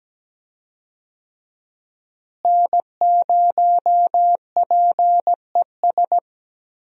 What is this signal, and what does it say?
Morse code, heard as a keyed tone, about 17 wpm, 700 Hz. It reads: N0PES
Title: USA Call Signs - 5 Characters - Repeated 3x - 17wpm